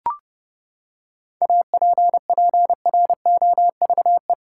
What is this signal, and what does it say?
Morse code, heard as a keyed tone, about 30 wpm, 700 Hz. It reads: APPROVE
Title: Seven Letter Words - 30wpm